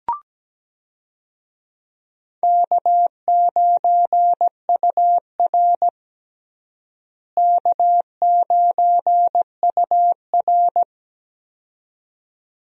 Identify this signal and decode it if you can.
Morse code, heard as a keyed tone, about 17 wpm, 700 Hz. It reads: K9UR K9UR
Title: USA Call Signs - 4 Characters - Repeated 2x - 17wpm